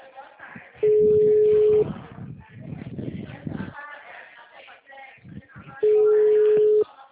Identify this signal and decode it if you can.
Ringback tone (Europe, 425 Hz), with speech.
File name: out- -2031- - - .64076